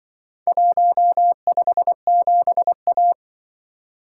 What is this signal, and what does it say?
Morse code, heard as a keyed tone, about 23 wpm, 700 Hz. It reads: 157A
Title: Speed-Racing - Four Letter-Number Combinations - 20wpm